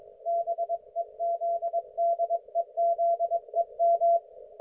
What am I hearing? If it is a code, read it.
Morse code, heard as a keyed tone, about 21 wpm, 680 Hz. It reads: BEZDEZEM
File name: Archiv / CUC216 / ok1fhi cul216